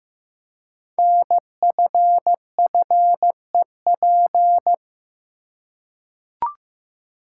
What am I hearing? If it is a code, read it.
Morse code, heard as a keyed tone, about 15 wpm, 700 Hz. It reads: NFFEP
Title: Code Groups - 5 Letters - 15wpm